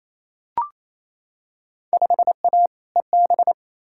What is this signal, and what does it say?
Morse code, heard as a keyed tone, about 28 wpm, 700 Hz. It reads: 5A EB
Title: Field Day Exchanges Encore - 28wpm